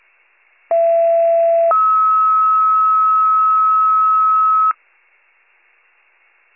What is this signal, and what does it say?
Two-tone page, heard as a one-second tone, then a three-second tone, about 670 then 1290 Hz. It is heard from 0.7 to 4.7 s.